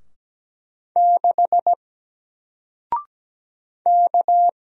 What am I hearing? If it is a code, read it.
Morse code, heard as a keyed tone, about 17 wpm, 700 Hz. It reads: 6 K
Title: Single Letter-Number Alt 17wpm